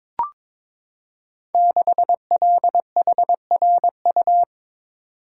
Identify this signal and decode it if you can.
Morse code, heard as a keyed tone, about 22 wpm, 700 Hz. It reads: 6LHRU
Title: Code Groups - 5 Letter:Numbers - 22wpm